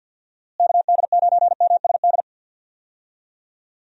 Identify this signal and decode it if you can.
Morse code, heard as a keyed tone, about 50 wpm, 700 Hz. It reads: KD9MSD
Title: USA Call Signs - 6 Characters - Repeated 2x - 50wpm